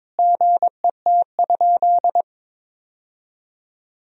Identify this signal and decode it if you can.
Morse code, heard as a keyed tone, about 21 wpm, 700 Hz. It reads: GET?
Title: Sentences from Top 1000 Words - Review Part 4 - 22wpm